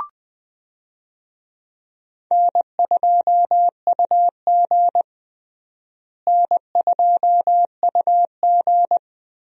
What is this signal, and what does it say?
Morse code, heard as a keyed tone, about 20 wpm, 700 Hz. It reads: N2UG N2UG
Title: USA Call Signs - 4 Characters - Repeated 3x - 20wpm